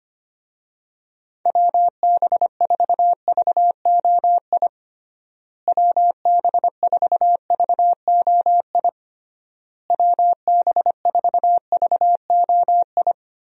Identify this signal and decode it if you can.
Morse code, heard as a keyed tone, about 25 wpm, 700 Hz. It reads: WB4VOI WB4VOI WB4VOI
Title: USA Call Signs - 6 Characters - Repeated 3x - 25wpm